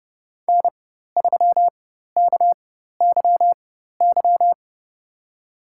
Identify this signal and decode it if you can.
Morse code, heard as a keyed tone, about 30 wpm, spaced as for 14 wpm, 690 Hz. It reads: N3KYY
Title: USA Call Signs - 4x Character Spacing - 30wpm